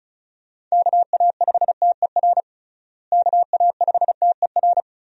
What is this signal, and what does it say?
Morse code, heard as a keyed tone, about 34 wpm, 700 Hz. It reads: KA5TER KA5TER
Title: USA Call Signs - 6 Characters - Repeated 3x - 35wpm